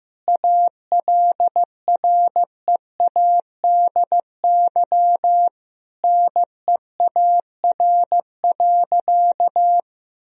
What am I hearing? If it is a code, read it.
Morse code, heard as a keyed tone, about 15 wpm, 700 Hz. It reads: ALREADY NEAR.
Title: Sentences from Top 1000 Words - Review Part 1 - 15wpm